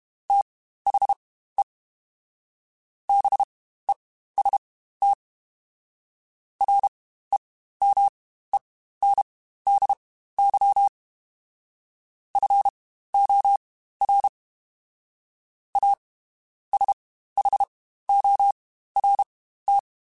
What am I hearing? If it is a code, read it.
Morse code, heard as a keyed tone, about 32 wpm, spaced as for 15 wpm, 800 Hz. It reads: THE BEST REMENDY FOR A SHORT
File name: Quotes Spaced / Quotes Spaced-mp3s / Temper sp @30